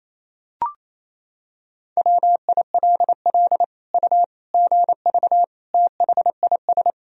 Morse code, heard as a keyed tone, about 28 wpm, 700 Hz. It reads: WILL U GV THIS
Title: Sentences from Top 200 Words and Abbreviations 28wpm